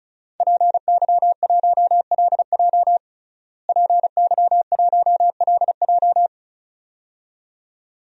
Morse code, heard as a keyed tone, about 36 wpm, 700 Hz. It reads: PY1LJ PY1LJ
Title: DXCC-100 Call Signs - 5 Characters - Repeated 2x - 35wpm